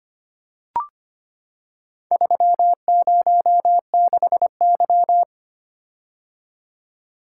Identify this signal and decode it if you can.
Morse code, heard as a keyed tone, about 25 wpm, 700 Hz. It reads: 306Y
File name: practice / 25wpm / Four Letter-Number Combinations - 25wpm